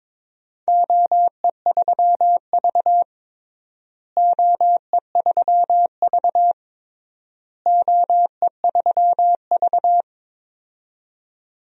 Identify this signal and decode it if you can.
Morse code, heard as a keyed tone, about 22 wpm, 700 Hz. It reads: OE3V OE3V OE3V
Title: DXCC-100 Call Signs - 4 Characters - Repeated 3x - 22wpm